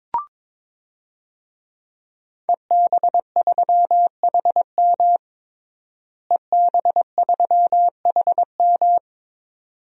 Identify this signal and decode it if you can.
Morse code, heard as a keyed tone, about 22 wpm, 700 Hz. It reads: EB3HM EB3HM
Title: DXCC-100 Call Signs - 5 Characters - Repeated 2x - 22wpm